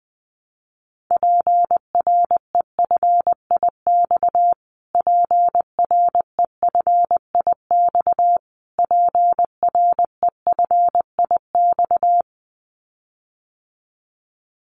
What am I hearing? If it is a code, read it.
Morse code, heard as a keyed tone, about 20 wpm, 700 Hz. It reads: PREFIX PREFIX PREFIX
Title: Callsign Prefixes for Top 25 DXCC - Repeated 3x - 20wpm